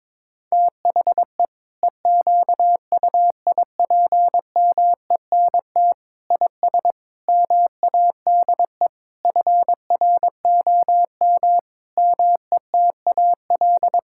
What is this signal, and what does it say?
Morse code, heard as a keyed tone, about 22 wpm, 700 Hz. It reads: THE EQUIPMENT IS MADE FROM METAL
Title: Sentences from Top 1000 Words - Review Part 6 - 22wpm